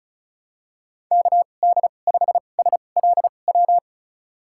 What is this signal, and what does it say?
Morse code, heard as a keyed tone, about 34 wpm, spaced as for 25 wpm, 700 Hz. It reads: KD5SLW
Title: USA Call Signs - 2x Character Spacing - 35wpm